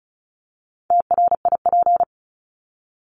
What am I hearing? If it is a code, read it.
Morse code, heard as a keyed tone, about 34 wpm, 700 Hz. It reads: TRIP
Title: Top 1251-1500 Words 35wpm